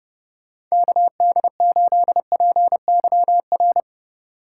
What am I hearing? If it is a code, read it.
Morse code, heard as a keyed tone, about 32 wpm, 700 Hz. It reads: KD8PYR